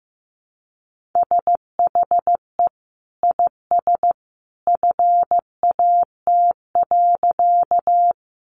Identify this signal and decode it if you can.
Morse code, heard as a keyed tone, about 14 wpm, 700 Hz. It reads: SHE IS FAT.
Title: Sentences from Top 600 Words 15wpm